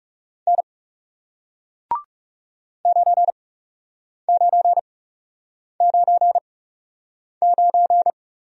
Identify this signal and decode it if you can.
Morse code, heard as a keyed tone, about 39 wpm, spaced as for 8 wpm, 700 Hz. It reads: N 9999